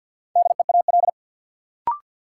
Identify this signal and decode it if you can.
Morse code, heard as a keyed tone, about 52 wpm, 700 Hz. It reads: DEAL
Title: Sets of 3 Words - New Word Drill - 50wpm